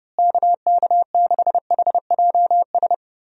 Morse code, heard as a keyed tone, about 32 wpm, 700 Hz. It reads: KK6HJS
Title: USA Call Signs - 6 Characters - Repeated 2x - 30wpm